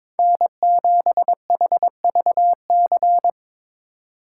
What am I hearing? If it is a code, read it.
Morse code, heard as a keyed tone, about 22 wpm, 700 Hz. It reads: N7HVC
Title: USA Call Signs - 3x Character Spacing - 22wpm